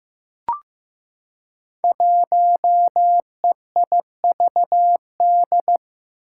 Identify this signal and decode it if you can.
Morse code, heard as a keyed tone, about 15 wpm, 700 Hz. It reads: 1EIVD
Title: Five Challenging Letter-Number Combinations 15wpm